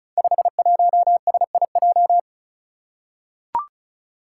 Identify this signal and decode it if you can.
Morse code, heard as a keyed tone, about 36 wpm, 700 Hz. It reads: 51SIJ
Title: Five Challenging Letter-Number Combinations 35wpm